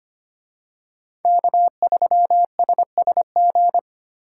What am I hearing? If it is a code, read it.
Morse code, heard as a keyed tone, about 24 wpm, 700 Hz. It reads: K3SSG
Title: Speed Racing - USA 1x3 Call Signs 25wpm